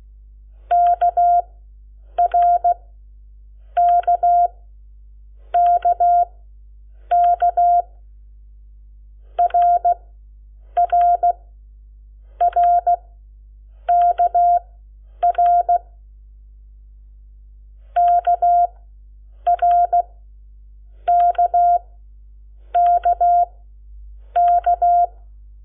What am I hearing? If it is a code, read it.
Morse code, heard as a keyed tone, about 16 wpm, spaced as for 7 wpm, 700 Hz. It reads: KRKKK RRRKR KRKKK
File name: Lecciones / CLASE Nº7